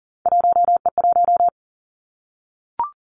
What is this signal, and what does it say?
Morse code, heard as a keyed tone, about 40 wpm, 700 Hz. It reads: 1E1
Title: Three Challenging Letter-Number Combinations 40wpm